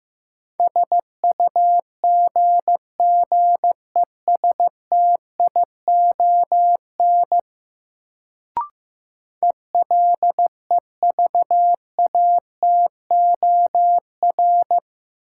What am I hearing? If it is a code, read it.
Morse code, heard as a keyed tone, about 15 wpm, 700 Hz. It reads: SUGGESTION ELEVATOR
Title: Sentences from Top 1600 Words - New Word Drill Alt - 15wpm